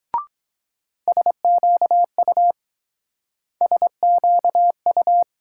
Morse code, heard as a keyed tone, about 22 wpm, 700 Hz. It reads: SQU SQU